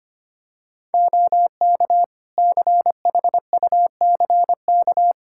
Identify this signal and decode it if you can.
Morse code, heard as a keyed tone, about 25 wpm, 700 Hz. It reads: OK CHUCK
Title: Practice QSO - Exchange 4 - Ending - 25wpm